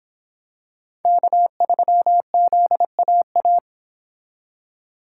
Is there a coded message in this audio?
Yes: Morse code from 1.1 to 3.6 s, heard as a keyed tone, about 25 wpm, 700 Hz. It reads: K3ZAA